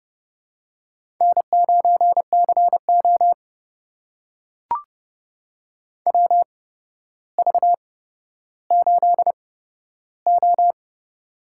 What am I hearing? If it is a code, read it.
Morse code, heard as a keyed tone, about 31 wpm, spaced as for 13 wpm, 700 Hz. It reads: N9CO WV8O
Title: USA Call Signs - 4 Characters - 8x Character Spacing - 30wpm